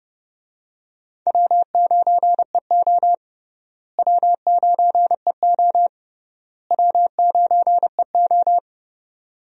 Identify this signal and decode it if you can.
Morse code, heard as a keyed tone, about 30 wpm, 700 Hz. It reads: W9EO W9EO W9EO
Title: USA Call Signs - 4 Characters - Repeated 3x - 30wpm